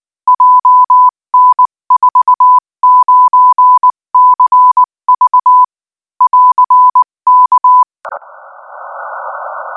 Morse code, heard as a keyed tone, about 19 wpm, 1000 Hz. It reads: JN49CV <AR>K
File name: IZ4BEH-13-cm-AS